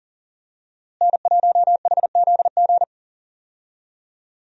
Morse code, heard as a keyed tone, about 37 wpm, 700 Hz. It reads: N1HZG
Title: USA Call Signs - 5 Characters - Repeated 3x - 40wpm